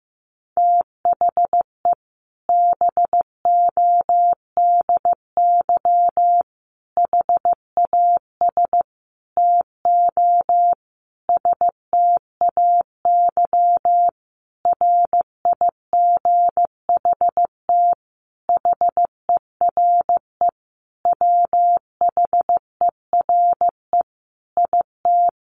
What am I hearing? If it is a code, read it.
Morse code, heard as a keyed tone, about 15 wpm, 700 Hz. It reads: THE BODY HAS TO STAY RIGHT HERE WHERE IT
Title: A Sound Of Thunder 15wpm